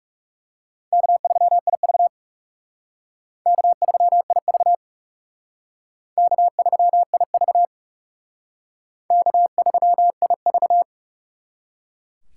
Morse code, heard as a keyed tone, about 36 wpm, 700 Hz. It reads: K3IV K3IV K3IV K3IV
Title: Speed Racing - USA 1x2 Call Signs 30wpm